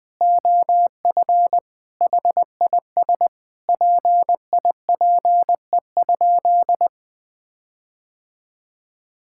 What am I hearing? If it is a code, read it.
Morse code, heard as a keyed tone, about 21 wpm, 700 Hz. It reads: OF HIS PIPE?
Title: Sentences from Top 1600 Words 20wpm